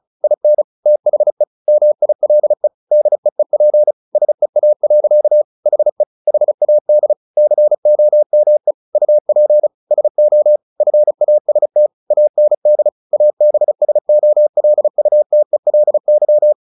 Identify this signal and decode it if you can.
Morse code, heard as a keyed tone, about 34 wpm, 600 Hz. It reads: IN THE MILE DEEP SEA. HE HAD COME UP SO FAST AND ABSOLUTELY
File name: the old man and the sea 2 35wpm